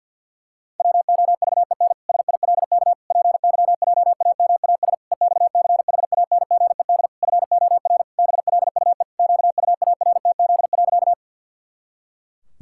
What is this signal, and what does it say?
Morse code, heard as a keyed tone, about 50 wpm, 700 Hz. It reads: WOVEN SILK PYJAMAS EXCHANGED FOR BLUE QUARTZ.
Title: Pangram Challenge v2 - 50wpm